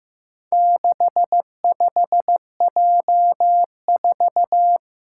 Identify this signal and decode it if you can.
Morse code, heard as a keyed tone, about 15 wpm, 700 Hz. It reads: 65J4
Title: Four Challenging Letter-Number Combinations 15wpm